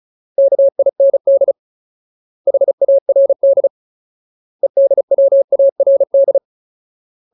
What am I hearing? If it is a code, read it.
Morse code, heard as a keyed tone, about 34 wpm, 550 Hz. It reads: KIND HARD EDWARD
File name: Ultimate Word List / 35WPM Ultimate